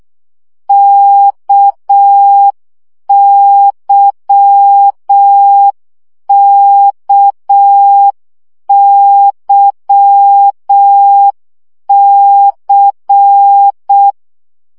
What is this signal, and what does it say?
Morse code, heard as a keyed tone, about 6 wpm, 800 Hz. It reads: KYKYC
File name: CW gyakorlas k c y 5 baud